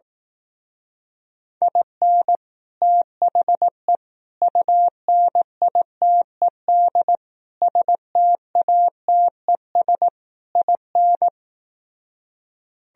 Morse code, heard as a keyed tone, about 18 wpm, 700 Hz. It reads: IN THE UNITED STATES IN
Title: Speed-Racing - Sets of 5 Words 15wpm